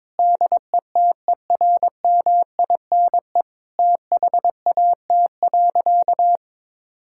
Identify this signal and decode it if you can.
Morse code, heard as a keyed tone, about 22 wpm, 700 Hz. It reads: DETERMINE THAT.
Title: Sentences from Top 1000 Words - Review Part 5 - 22wpm